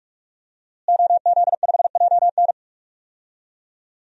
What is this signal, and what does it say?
Morse code, heard as a keyed tone, about 42 wpm, 700 Hz. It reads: OZ5JN
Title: DXCC-100 Call Signs - 5 Characters - Repeated 3x - 45wpm